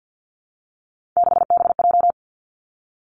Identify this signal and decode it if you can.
Morse code, heard as a keyed tone, about 50 wpm, 700 Hz. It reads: 6BJ
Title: Three Challenging Letter-Number Combinations 50wpm